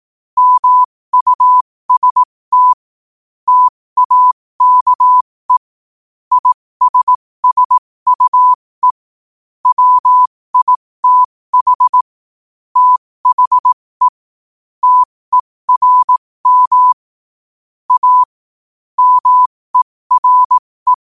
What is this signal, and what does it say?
Morse code, heard as a keyed tone, about 18 wpm, spaced as for 15 wpm, 1000 Hz. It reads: MUST TAKE ISSUE WITH THE TERM A MERE
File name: 15wpm